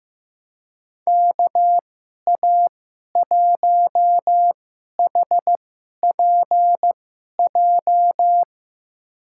Morse code, heard as a keyed tone, about 15 wpm, spaced as for 11 wpm, 700 Hz. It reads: KA1HPJ
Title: USA Call Signs - 2x Character Spacing - 15wpm